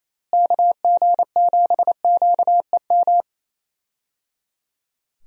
Morse code, heard as a keyed tone, about 28 wpm, 700 Hz. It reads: KG7QEM
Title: USA Call Signs - 6 Characters - Repeated 3x - 28wpm